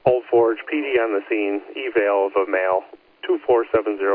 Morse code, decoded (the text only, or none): none